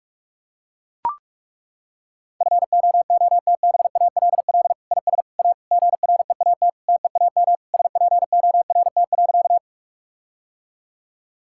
Morse code, heard as a keyed tone, about 42 wpm, 700 Hz. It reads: FOOTBALL IS A GREAT TEAM SPORT.
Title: Sentences from Top 1600 Words 45wpm